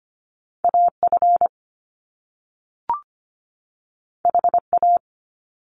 Morse code, heard as a keyed tone, about 25 wpm, 700 Hz. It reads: AF HA
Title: Top Two Letter Combinations in Words 25wpm